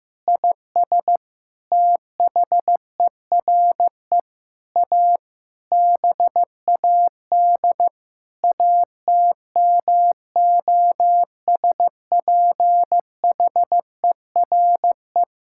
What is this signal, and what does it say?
Morse code, heard as a keyed tone, about 15 wpm, 700 Hz. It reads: IS THERE A BAD ATMOSPHERE
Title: Sentences from Top 1600 Words 15wpm